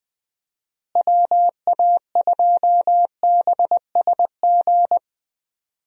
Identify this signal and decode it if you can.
Morse code, heard as a keyed tone, about 20 wpm, 700 Hz. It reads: WA2BSG
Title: USA Call Signs - 6 Characters - Repeated 2x - 20wpm